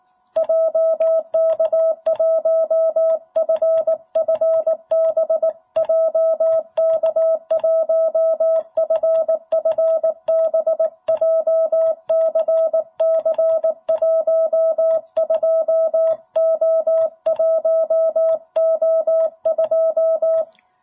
Morse code, heard as a keyed tone, about 18 wpm, 660 Hz. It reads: JK1FFB JK1FFB JCC 12O1O2
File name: jh8cbh / 2021cw